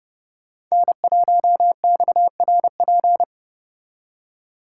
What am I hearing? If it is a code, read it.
Morse code, heard as a keyed tone, about 29 wpm, 700 Hz. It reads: N1XRP